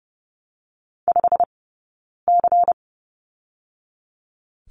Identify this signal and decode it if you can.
Morse code, heard as a keyed tone, about 30 wpm, 700 Hz. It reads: 5 C
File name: course / 30:10wpm / 099 Lesson 17d - Letter u - Two Characters 30wpm